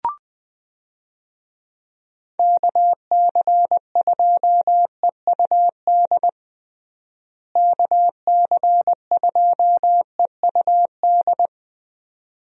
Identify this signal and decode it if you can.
Morse code, heard as a keyed tone, about 20 wpm, 700 Hz. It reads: KC2EUD KC2EUD
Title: USA Call Signs - 6 Characters - Repeated 2x - 20wpm